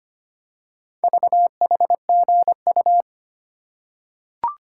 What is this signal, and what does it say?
Morse code, heard as a keyed tone, about 25 wpm, 700 Hz. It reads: VHGU
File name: practice / 25wpm / Four Letter-Number Combinations - 25wpm